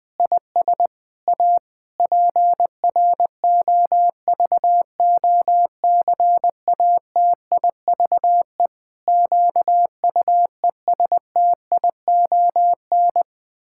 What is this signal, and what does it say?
Morse code, heard as a keyed tone, about 21 wpm, 700 Hz. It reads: IS A PROVOCATIVE QUESTION
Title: Pangram Challenge v2 - 20wpm